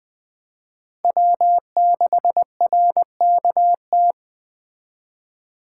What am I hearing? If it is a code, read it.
Morse code, heard as a keyed tone, about 24 wpm, 700 Hz. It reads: W6RKT
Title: Speed Racing - USA Call Signs 20wpm